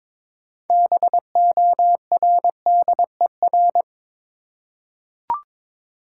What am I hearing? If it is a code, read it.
Morse code, heard as a keyed tone, about 22 wpm, 700 Hz. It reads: BORDER
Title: Six Letter Words - 22wpm